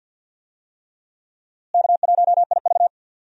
Morse code, heard as a keyed tone, about 51 wpm, 700 Hz. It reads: K1IV
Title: USA Call Signs - 4 Characters - Repeated 3x - 50wpm